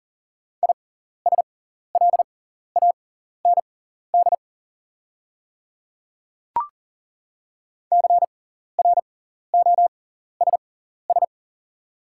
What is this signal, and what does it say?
Morse code, heard as a keyed tone, about 40 wpm, spaced as for 14 wpm, 700 Hz. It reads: ISLAND CROSS
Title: 6x Character Spacing - Top 500 Words - 40wpm